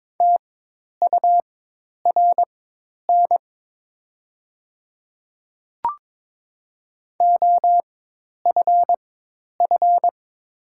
Morse code, heard as a keyed tone, about 22 wpm, spaced as for 10 wpm, 700 Hz. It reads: TURN OFF
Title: 4x Character Spacing - Top 300 Words - 22wpm